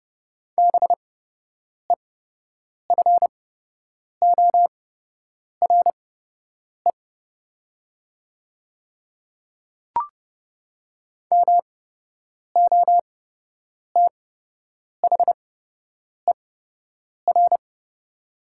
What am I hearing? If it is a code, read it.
Morse code, heard as a keyed tone, about 29 wpm, spaced as for 8 wpm, 700 Hz. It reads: BEFORE MOTHER
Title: 8x Character Spacing - Top 300 Words - 30wpm